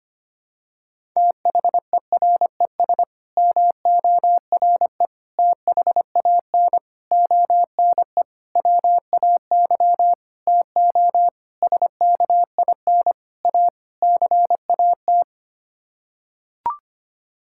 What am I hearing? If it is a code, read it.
Morse code, heard as a keyed tone, about 25 wpm, 700 Hz. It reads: THERES MORE THAN ONE WAY TO SKIN A CAT
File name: Sets of 9 Words - 25wpm